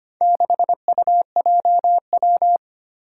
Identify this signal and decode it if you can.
Morse code, heard as a keyed tone, about 25 wpm, 700 Hz. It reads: 6UJW
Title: Four Challenging Letter-Number Combinations 25wpm